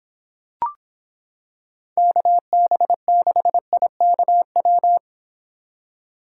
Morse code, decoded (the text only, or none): KB6IKW